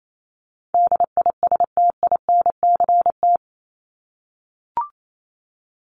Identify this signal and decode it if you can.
Morse code, heard as a keyed tone, about 28 wpm, 700 Hz. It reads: DISTINCT
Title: Top 1501-1750 Words 28wpm